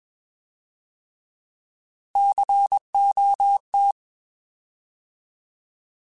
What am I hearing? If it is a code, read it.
Morse code, heard as a keyed tone, about 21 wpm, 800 Hz. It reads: COT
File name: Alphabetical / Alphabetical mp3s / Cs @20